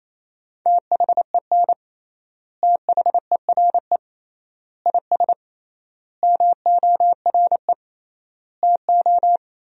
Morse code, heard as a keyed tone, about 28 wpm, 700 Hz. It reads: THEN THERE IS MORE TO
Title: 3x Word-Spacing - Sentences from Top 100 Words - 28wpm